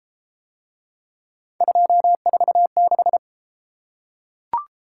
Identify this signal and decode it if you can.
Morse code, heard as a keyed tone, about 30 wpm, 700 Hz. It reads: 246